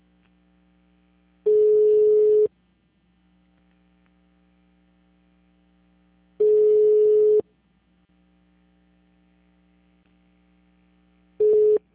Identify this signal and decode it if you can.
Ringback tone (Europe, 425 Hz).